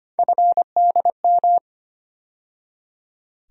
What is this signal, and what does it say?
Morse code, heard as a keyed tone, about 25 wpm, 700 Hz. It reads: FDM
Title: USA Call Sign Postfixes - 3 Letters 25wpm